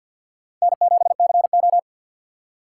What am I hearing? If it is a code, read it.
Morse code, heard as a keyed tone, about 47 wpm, 700 Hz. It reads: N7CO
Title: USA Call Signs - 4 Characters - 3x Character Spacing - 50wpm